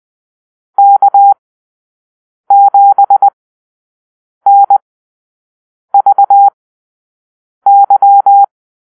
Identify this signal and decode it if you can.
Morse code, heard as a keyed tone, about 20 wpm, spaced as for 6 wpm, 800 Hz. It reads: K7NVY
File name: arrl / Call-Sign Drill-6